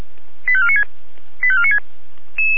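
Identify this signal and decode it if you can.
Five-tone selective call (ZVEI), heard as quick runs of five tones, tones of about 80 ms. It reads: 85496 85496
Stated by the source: ZVEI five-tone selective call received with an SDR.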